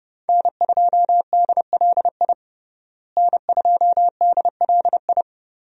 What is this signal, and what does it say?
Morse code, heard as a keyed tone, about 30 wpm, 700 Hz. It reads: N2DLI N2DLI
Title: USA Call Signs - 5 Characters - Repeated 2x - 30wpm